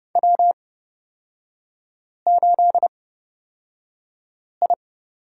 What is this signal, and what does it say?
Morse code, heard as a keyed tone, about 30 wpm, spaced as for 5 wpm, 700 Hz. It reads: W8I
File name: Lesson 38e - Letter q - Three Characters 30wpm